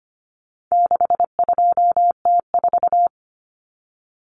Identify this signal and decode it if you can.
Morse code, heard as a keyed tone, about 25 wpm, 700 Hz. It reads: 62T4
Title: Common Cut Numbers - 4-Digit 25wpm